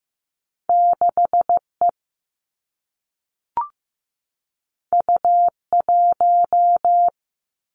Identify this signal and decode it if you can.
Morse code, heard as a keyed tone, about 15 wpm, 700 Hz. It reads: 6E U1
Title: Two Challenging Letter-Number Combinations 15wpm